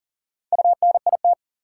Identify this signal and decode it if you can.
Morse code, heard as a keyed tone, about 40 wpm, 700 Hz. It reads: UNIT
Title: Four Letter Words - 40wpm